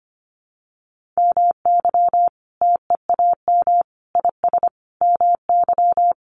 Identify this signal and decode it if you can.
Morse code, heard as a keyed tone, about 25 wpm, 700 Hz. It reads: MY TEAM IS MY
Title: Sentences from Top 1500 Words 25wpm